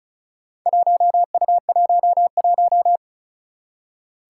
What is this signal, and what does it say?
Morse code, heard as a keyed tone, about 36 wpm, 700 Hz. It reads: 1U11
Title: Four Challenging Letter-Number Combinations 35wpm